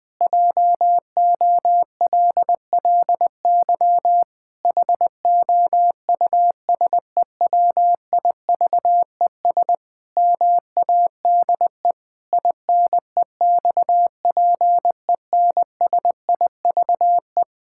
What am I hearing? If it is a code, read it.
Morse code, heard as a keyed tone, about 20 wpm, 700 Hz. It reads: JOLLY HOUSEWIVES MADE INEXPENSIVE
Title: Pangram Challenge v2 - 20wpm